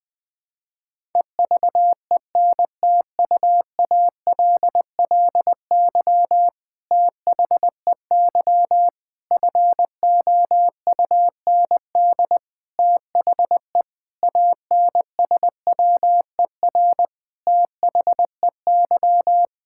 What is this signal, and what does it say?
Morse code, heard as a keyed tone, about 21 wpm, 700 Hz. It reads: EVENTUALLY THEY FOUND THE ANSWER THEY
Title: Sentences from Top 1000 Words - Review Part 6 - 20wpm